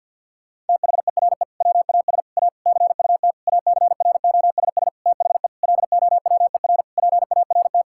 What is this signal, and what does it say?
Morse code, heard as a keyed tone, about 50 wpm, 700 Hz. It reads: THERE WAS A CUT ACROSS THE LOWER PART
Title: Sentences from Top 1000 Words - Review Part 5 - 50wpm